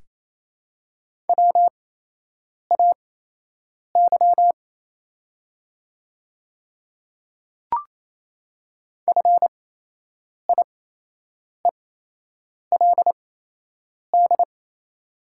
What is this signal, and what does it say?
Morse code, heard as a keyed tone, about 28 wpm, spaced as for 8 wpm, 700 Hz. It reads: WAY FIELD
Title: 8x Character Spacing - Top 500 Words - 28wpm